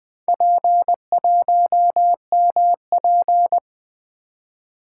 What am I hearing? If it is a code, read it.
Morse code, heard as a keyed tone, about 20 wpm, 700 Hz. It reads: P1MP
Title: Four Letter-Number Combinations - 20wpm